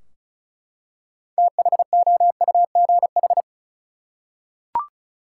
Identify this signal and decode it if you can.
Morse code, heard as a keyed tone, about 36 wpm, 700 Hz. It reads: THOUGH
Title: Six Letter Words - 35wpm